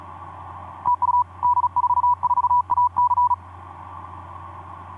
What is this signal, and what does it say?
Morse code, heard as a keyed tone, about 40 wpm, 980 Hz. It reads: TU D44AC
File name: d44ac ek2rgt etc 10m cw 23nov2010 1202